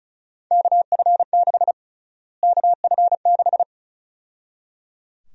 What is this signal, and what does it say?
Morse code, heard as a keyed tone, about 34 wpm, 700 Hz. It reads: KF6 KF6
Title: USA Call Sign Prefixes - Repeated 2x - 35wpm